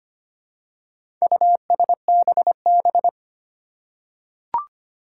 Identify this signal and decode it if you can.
Morse code, heard as a keyed tone, about 25 wpm, 700 Hz. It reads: USBB
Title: Four Challenging Letter-Number Combinations 25wpm